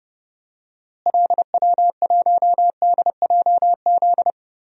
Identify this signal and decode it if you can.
Morse code, heard as a keyed tone, about 31 wpm, 700 Hz. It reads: LW1DJZ
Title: DXCC-100 Call Signs - 6 Characters - Repeated 2x - 30wpm